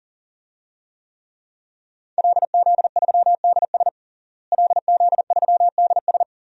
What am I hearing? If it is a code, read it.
Morse code, heard as a keyed tone, about 40 wpm, 700 Hz. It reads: LZ3DS LZ3DS
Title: DXCC-100 Call Signs - 5 Characters - Repeated 2x - 40wpm